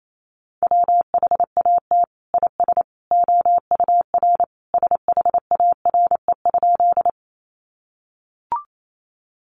Morse code, heard as a keyed tone, about 28 wpm, 700 Hz. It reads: WHAT IS OUR SHARE?
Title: Sentences from Top 900 Words 28wpm